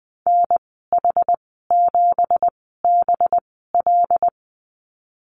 Morse code, heard as a keyed tone, about 20 wpm, spaced as for 14 wpm, 700 Hz. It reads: NH7BL
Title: 213 Lesson 34g - Number 8 - Call Signs 20wpm